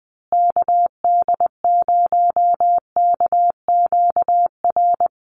Morse code, heard as a keyed tone, about 20 wpm, 700 Hz. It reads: KD0KQR
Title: USA Call Signs Repeated 2x 20wpm